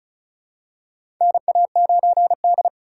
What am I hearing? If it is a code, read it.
Morse code, heard as a keyed tone, about 36 wpm, 700 Hz. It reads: NA9D
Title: USA Call Signs - 4 Characters - 3x Character Spacing - 35wpm